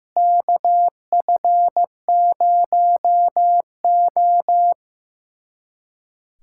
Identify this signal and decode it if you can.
Morse code, heard as a keyed tone, about 15 wpm, 700 Hz. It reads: KF0O
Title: USA Callsign DE Callsign - 4 Characters - 15wpm